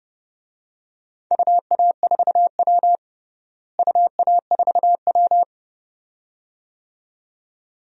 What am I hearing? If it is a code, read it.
Morse code, heard as a keyed tone, about 30 wpm, 700 Hz. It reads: UA4W UA4W
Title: DXCC-100 Call Signs - 4 Characters - Repeated 2x - 30wpm